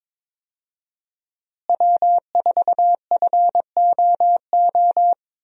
Morse code, heard as a keyed tone, about 21 wpm, 700 Hz. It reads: W4FOO
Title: USA Call Signs - 5 Characters - Repeated 3x - 22wpm